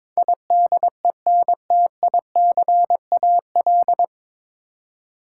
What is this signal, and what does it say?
Morse code, heard as a keyed tone, about 21 wpm, 700 Hz. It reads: IDENTICAL
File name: 9 Letter Words - First 3 Letters Given - 22wpm